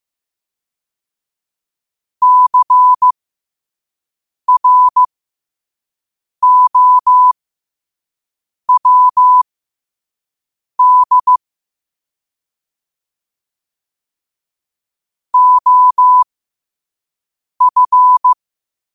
Morse code, heard as a keyed tone, about 15 wpm, spaced as for 5 wpm, 1000 Hz. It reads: CROWD OF
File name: qotdfiles / 5wpm